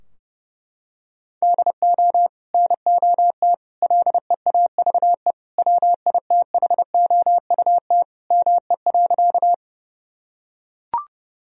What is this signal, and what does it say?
Morse code, heard as a keyed tone, about 30 wpm, 700 Hz. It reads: DO NOT LEAVE WITHOUT ME.
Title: Sentences from Top 1000 Words - Review Part 3 - 30wpm